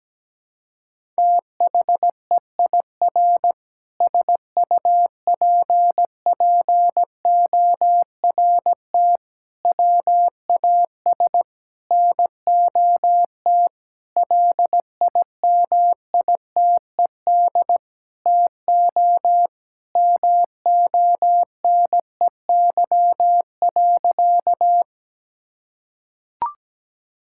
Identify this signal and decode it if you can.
Morse code, heard as a keyed tone, about 17 wpm, 700 Hz. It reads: THEIR SUPPORT WAS NOT LIMITED TO MONEY.
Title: Sentences from Top 1000 Words - Review Part 6 - 17wpm